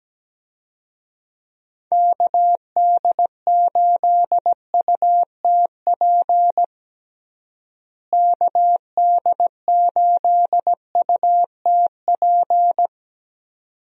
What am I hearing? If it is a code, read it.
Morse code, heard as a keyed tone, about 17 wpm, 700 Hz. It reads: KD8UTP KD8UTP
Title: USA Call Signs - 6 Characters - Repeated 2x - 17wpm